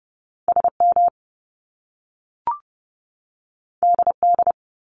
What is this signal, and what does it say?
Morse code, heard as a keyed tone, about 30 wpm, 700 Hz. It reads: SM DD
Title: Top Two Letter Combinations in Words 30wpm